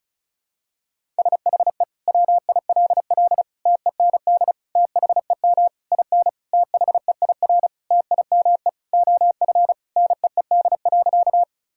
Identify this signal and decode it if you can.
Morse code, heard as a keyed tone, about 36 wpm, 700 Hz. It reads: SHE WILL TEND THEM IN THEIR TIME OF NEED.
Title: Sentences from Top 1000 Words - Review Part 5 - 35wpm